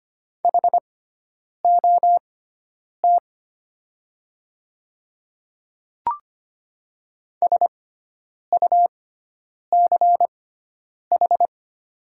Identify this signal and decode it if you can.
Morse code, heard as a keyed tone, about 25 wpm, spaced as for 9 wpm, 700 Hz. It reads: HOT SUCH
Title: 6x Character Spacing - Top 200 Words - 25wpm